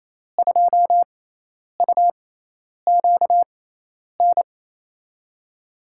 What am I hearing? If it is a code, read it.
Morse code, heard as a keyed tone, about 28 wpm, spaced as for 10 wpm, 700 Hz. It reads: 2UQN
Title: USA Call Signs - 6 Characters - 6x Character Spacing - 28wpm